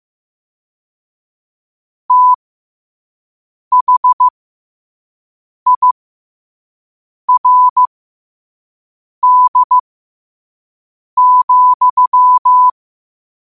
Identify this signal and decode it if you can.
Morse code, heard as a keyed tone, about 15 wpm, spaced as for 5 wpm, 1000 Hz. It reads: THIRD,
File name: qotdfiles / 5wpm